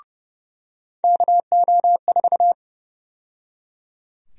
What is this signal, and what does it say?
Morse code, heard as a keyed tone, about 30 wpm, 700 Hz. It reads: KO4
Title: USA Call Sign Prefixes - 30wpm